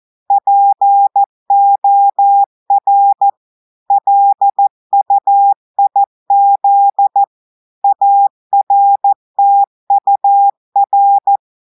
Morse code, heard as a keyed tone, about 14 wpm, 800 Hz. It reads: POR LUIZ ARTUR
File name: audios / portugais / portugais texte en clair 1 vitesse 14 mots minutes